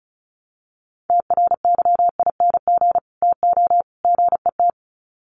Morse code, heard as a keyed tone, about 34 wpm, 700 Hz. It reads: TRYING TO GET
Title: Sets of 3 Words 35wpm